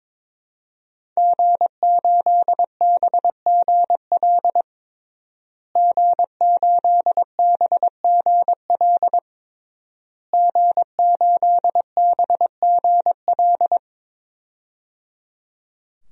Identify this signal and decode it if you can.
Morse code, heard as a keyed tone, about 22 wpm, 700 Hz. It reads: G8BGL G8BGL G8BGL
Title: DXCC-100 Call Signs - 5 Characters - Repeated 3x - 22wpm